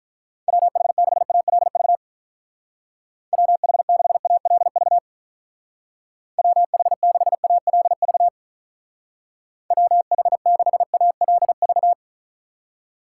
Morse code, heard as a keyed tone, about 44 wpm, 700 Hz. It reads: WH6ALV WH6ALV WH6ALV WH6ALV